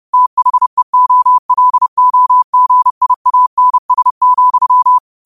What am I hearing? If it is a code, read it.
Morse code, heard as a keyed tone, about 30 wpm, 1000 Hz. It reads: THEOLOGIANS,
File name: qotdfiles / 30wpm